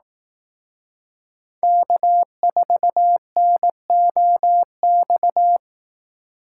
Morse code, heard as a keyed tone, about 18 wpm, 700 Hz. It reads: K4NOX
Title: Speed Racing - USA Call Signs 15wpm